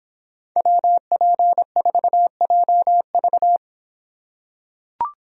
Morse code, heard as a keyed tone, about 24 wpm, 700 Hz. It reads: WP4JV